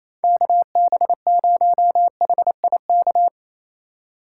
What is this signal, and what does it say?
Morse code, heard as a keyed tone, about 35 wpm, 700 Hz. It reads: KB0HIK
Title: Speed Racing - USA 2x3 Call Signs 28wpm